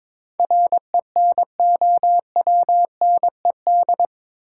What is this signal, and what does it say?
Morse code, heard as a keyed tone, about 22 wpm, 700 Hz. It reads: RENOWNED
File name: practice / 22wpm / Eight Letter Words - 22wpm